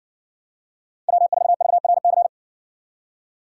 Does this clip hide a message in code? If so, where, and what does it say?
Yes: Morse code from 1.1 to 2.3 s, heard as a keyed tone, about 60 wpm, 700 Hz. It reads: W4VRC